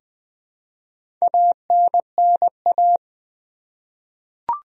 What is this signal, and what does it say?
Morse code, heard as a keyed tone, about 20 wpm, 700 Hz. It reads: ANNA
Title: USA Common Names - 20wpm